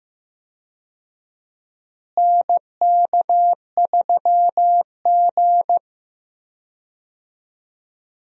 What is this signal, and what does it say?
Morse code, heard as a keyed tone, about 15 wpm, 700 Hz. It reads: NK3G
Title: USA Call Signs - 4 Characters - Repeated 2x - 15wpm